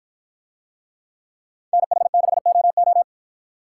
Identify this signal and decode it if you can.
Morse code, heard as a keyed tone, about 53 wpm, 700 Hz. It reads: NH6OO